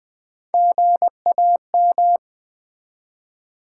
Morse code, heard as a keyed tone, about 20 wpm, 700 Hz. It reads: GAM
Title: Build A Word - 1 Letter At A Time - 3 to 5-Letter Words - 20wpm